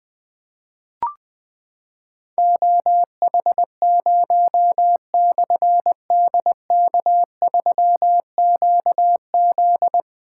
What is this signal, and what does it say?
Morse code, heard as a keyed tone, about 20 wpm, 700 Hz. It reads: OH0/DK3QZ
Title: Difficult Call Signs 20wpm